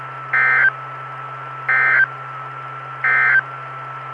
Emergency Alert System (SAME) tones, heard as harsh buzzing bursts. It is heard from 0.2 to 3.5 s.
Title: Saline, AK SKYWARN Repeater System - W5RHS